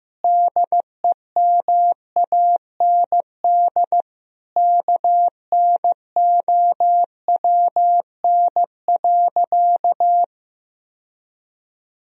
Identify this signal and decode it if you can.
Morse code, heard as a keyed tone, about 15 wpm, 700 Hz. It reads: DEMAND KNOWN.
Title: Sentences from Top 1000 Words - Review Part 4 - 15wpm